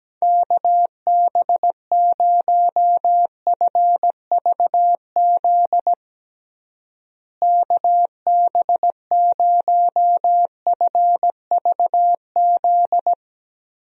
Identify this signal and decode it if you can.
Morse code, heard as a keyed tone, about 17 wpm, 700 Hz. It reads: KB0FVZ KB0FVZ
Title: USA Call Signs - 6 Characters - Repeated 3x - 17wpm